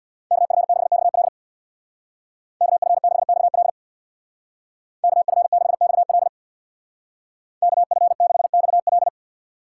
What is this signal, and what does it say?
Morse code, heard as a keyed tone, about 60 wpm, 700 Hz. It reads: KF6XL KF6XL KF6XL KF6XL
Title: Speed Racing - USA 2x2 Call Signs 50wpm